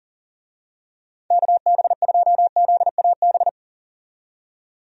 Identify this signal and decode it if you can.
Morse code, heard as a keyed tone, about 40 wpm, 700 Hz. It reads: KB2ZAB
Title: USA Call Signs - 6 Characters - 40wpm